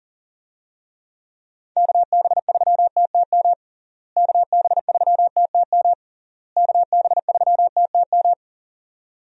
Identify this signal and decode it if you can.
Morse code, heard as a keyed tone, about 40 wpm, 700 Hz. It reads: KB3TTM KB3TTM KB3TTM
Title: USA Call Signs - 6 Characters - Repeated 3x - 40wpm